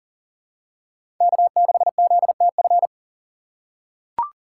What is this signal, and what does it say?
Morse code, heard as a keyed tone, about 40 wpm, 700 Hz. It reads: K6ZTF
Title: USA Call Signs - 5 Characters - 6x Character Spacing - 40wpm